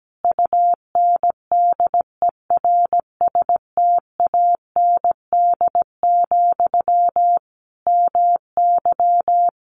Morse code, heard as a keyed tone, about 17 wpm, 700 Hz. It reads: UNDERSTAND, MY
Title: Sentences from Top 800 Words 17wpm